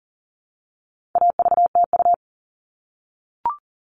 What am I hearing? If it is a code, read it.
Morse code, heard as a keyed tone, about 40 wpm, 700 Hz. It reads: AVTU
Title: All Cut Numbers - 4-Digit 40wpm